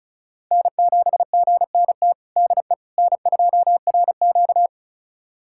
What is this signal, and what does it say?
Morse code, heard as a keyed tone, about 36 wpm, 700 Hz. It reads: N7GNT DE N2RQ
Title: USA Callsign DE Callsign 35wpm